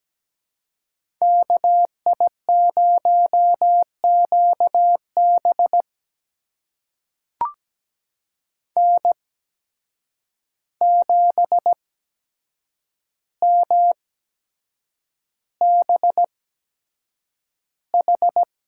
Morse code, heard as a keyed tone, about 17 wpm, spaced as for 7 wpm, 700 Hz. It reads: KI0QB N7MBH
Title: USA Call Signs - 5 Characters - 8x Character Spacing - 17wpm